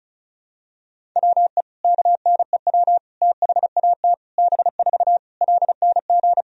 Morse code, heard as a keyed tone, about 34 wpm, 700 Hz. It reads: WE KNEW THAT B4 LNG
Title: Sentences from Top 400 Words and Abbreviations 35wpm